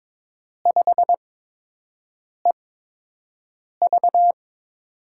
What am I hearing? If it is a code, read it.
Morse code, heard as a keyed tone, about 22 wpm, spaced as for 6 wpm, 700 Hz. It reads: 5EV
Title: USA Call Signs - 4 Characters - 8x Character Spacing - 22wpm